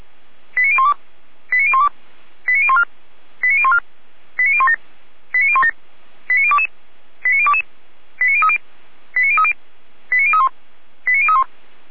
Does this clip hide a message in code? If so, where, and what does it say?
Yes: five-tone selective call (ZVEI) from 0.6 to 11.4 s, heard as quick runs of five tones, tones of about 70 ms. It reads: 89012 89012 89014 89014 89017 89017 89022 89022 89030 89030 89031 89031